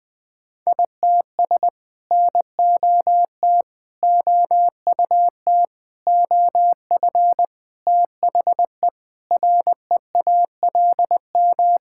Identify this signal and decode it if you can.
Morse code, heard as a keyed tone, about 20 wpm, 700 Hz. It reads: ITS NOT OUT OF THE REALM
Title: Sets of 9 Words - 20wpm